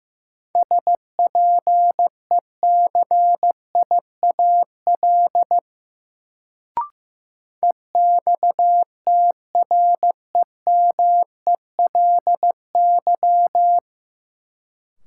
Morse code, heard as a keyed tone, about 15 wpm, 700 Hz. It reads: SPECIAL EXTREMELY
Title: Sentences from Top 1200 Words - New Word Drill Alt - 15wpm